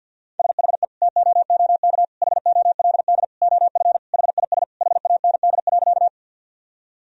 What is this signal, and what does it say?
Morse code, heard as a keyed tone, about 51 wpm, 700 Hz. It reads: SHE TOOK HOLD OF HIS HAND.
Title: Sentences from Top 1000 Words - Review Part 2 - 50wpm